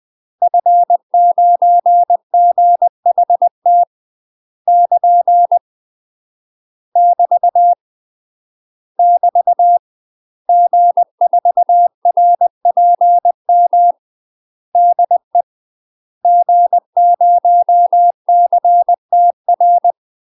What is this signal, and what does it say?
Morse code, heard as a keyed tone, about 20 wpm, 700 Hz. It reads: F9GHT <KN> <BT> <BT> G4RPM DE G0CTR
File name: UK MASTER QSOs TRAINING / 20WPM / UK MASTER QSOs 0030